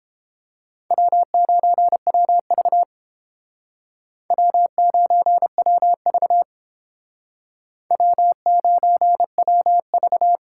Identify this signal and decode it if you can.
Morse code, heard as a keyed tone, about 30 wpm, 700 Hz. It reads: W9WV W9WV W9WV